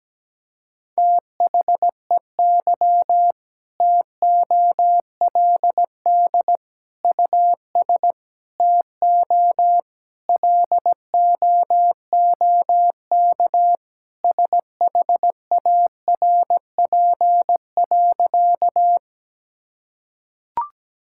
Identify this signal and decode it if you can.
Morse code, heard as a keyed tone, about 17 wpm, 700 Hz. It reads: THEY TOLD US TO LOOK SHARP.
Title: Sentences from Top 1600 Words 17wpm